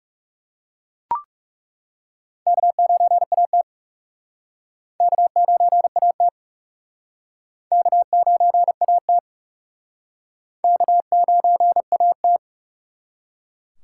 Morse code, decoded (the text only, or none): K9AT K9AT K9AT K9AT